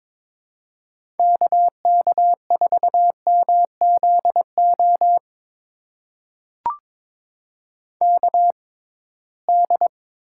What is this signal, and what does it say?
Morse code, heard as a keyed tone, about 23 wpm, spaced as for 17 wpm, 700 Hz. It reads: KK4MZO KD
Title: USA Call Signs - 6 Characters - 6x Character Spacing - 22wpm